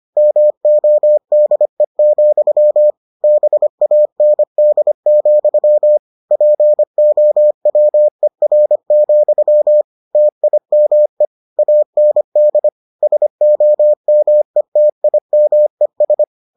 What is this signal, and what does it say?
Morse code, heard as a keyed tone, about 25 wpm, 600 Hz. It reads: MODE, BAND, POWER, TIME AND SOMETIMES